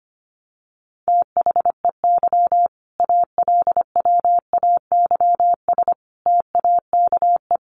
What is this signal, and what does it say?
Morse code, heard as a keyed tone, about 25 wpm, 700 Hz. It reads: THEY ALWAYS TAKE
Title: Sentences from Top 900 Words 25wpm